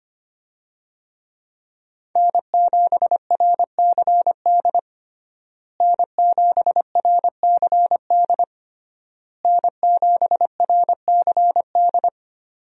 Morse code, heard as a keyed tone, about 25 wpm, 700 Hz. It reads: N7RCD N7RCD N7RCD
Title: USA Call Signs - 5 Characters - Repeated 3x - 25wpm